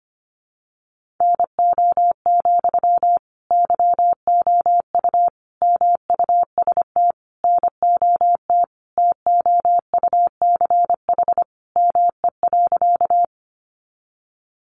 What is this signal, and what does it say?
Morse code, heard as a keyed tone, about 25 wpm, 700 Hz. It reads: NO, YOU MUST NOT TOUCH ME.
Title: Sentences from Top 1200 Words 25wpm